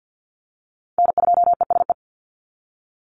Morse code, heard as a keyed tone, about 50 wpm, 700 Hz. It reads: N2ESE
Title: NA SOTA Chaser Callsigns 50wpm